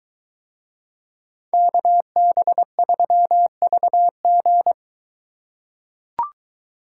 Morse code, heard as a keyed tone, about 21 wpm, 700 Hz. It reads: KB3VG